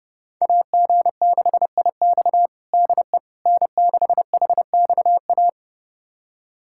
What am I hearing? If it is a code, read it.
Morse code, heard as a keyed tone, about 30 wpm, 700 Hz. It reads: AG6IX DE N6HXA
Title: USA Callsign DE Callsign - 5 Characters - 30wpm